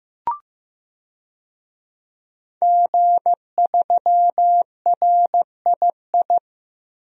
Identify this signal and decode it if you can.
Morse code, heard as a keyed tone, about 15 wpm, 700 Hz. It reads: G3RII
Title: DXCC-100 Call Signs - 5 Characters - 15wpm